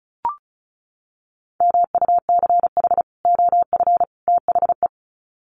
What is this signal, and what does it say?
Morse code, heard as a keyed tone, about 34 wpm, 700 Hz. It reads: MUCH OF THE
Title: Sets of 3 Words 35wpm